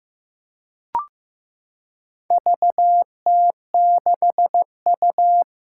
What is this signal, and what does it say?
Morse code, heard as a keyed tone, about 15 wpm, 700 Hz. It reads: VT6U
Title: Four Challenging Letter-Number Combinations 15wpm